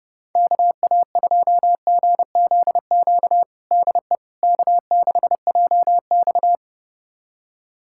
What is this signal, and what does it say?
Morse code, heard as a keyed tone, about 30 wpm, 700 Hz. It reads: KA2GZQ DE K6JX
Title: USA Callsign DE Callsign 30wpm